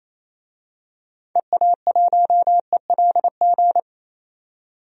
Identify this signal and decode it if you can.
Morse code, heard as a keyed tone, about 27 wpm, 700 Hz. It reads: EA1ELG
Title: DXCC-100 Call Signs - 6 Characters - 28wpm